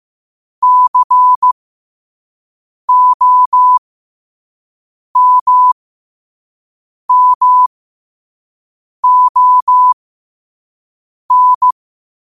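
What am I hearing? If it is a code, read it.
Morse code, heard as a keyed tone, about 15 wpm, spaced as for 5 wpm, 1000 Hz. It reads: COMMON
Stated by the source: Morse quote of the day at 5 wpm.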